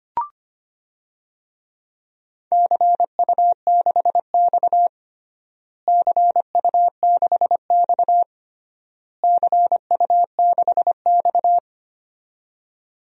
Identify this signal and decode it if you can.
Morse code, heard as a keyed tone, about 25 wpm, 700 Hz. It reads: CU6X CU6X CU6X
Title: DXCC-100 Call Signs - 4 Characters - Repeated 3x - 25wpm